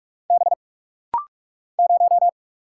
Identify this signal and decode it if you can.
Morse code, heard as a keyed tone, about 44 wpm, 700 Hz. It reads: B 0
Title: Single Letter-Number Alt - Rapid-Fire 45wpm